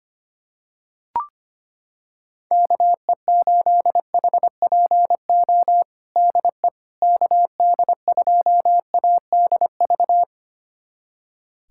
Morse code, heard as a keyed tone, about 25 wpm, 700 Hz. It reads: KE8HPO DE KD2ADV
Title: USA Callsign DE Callsign - 6 Characters - 25wpm